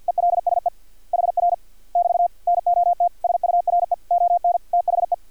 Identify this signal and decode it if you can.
Morse code, heard as a keyed tone, about 50 wpm, 700 Hz. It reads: ELSE HR <BT> NOT SURE ON THE
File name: archived / CRSnet